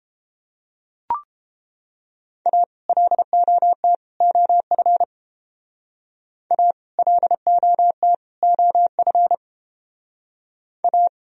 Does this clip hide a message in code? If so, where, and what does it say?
Yes: Morse code from 2.5 to 11.1 s, heard as a keyed tone, about 30 wpm, 700 Hz. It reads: A LOT OF A LOT OF A